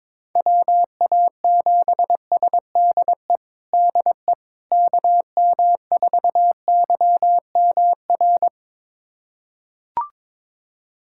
Morse code, heard as a keyed tone, about 22 wpm, 700 Hz. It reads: WA7SDE DE KM4YMR
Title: USA Callsign DE Callsign - 6 Characters - 22wpm